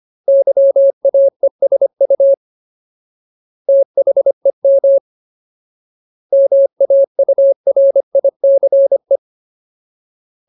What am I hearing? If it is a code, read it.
Morse code, heard as a keyed tone, about 25 wpm, 550 Hz. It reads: YAESU THEM MAURICE
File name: Ultimate Word List / 25WPM Ultimate